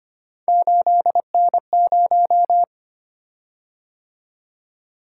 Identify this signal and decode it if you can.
Morse code, heard as a keyed tone, about 25 wpm, 700 Hz. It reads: 8N0
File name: Three Letter-Number Combinations - 25wpm